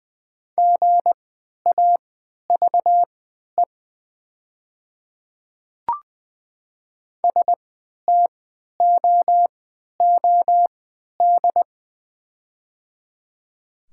Morse code, heard as a keyed tone, about 20 wpm, spaced as for 11 wpm, 700 Hz. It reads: GAVE STOOD
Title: 3x Character Spacing - Top 500 Words - 20wpm